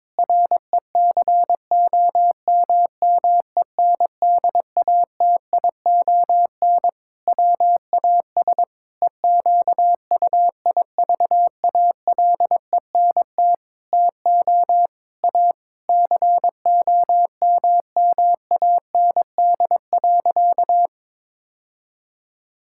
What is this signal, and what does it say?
Morse code, heard as a keyed tone, about 22 wpm, 700 Hz. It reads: RECOMMENDATION WAS EQUIVALENT TO A COMMAND.
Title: Sentences from Top 1600 Words 22wpm